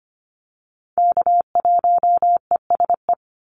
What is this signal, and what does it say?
Morse code, heard as a keyed tone, about 25 wpm, 700 Hz. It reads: K1ESE
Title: Call Signs - 25wpm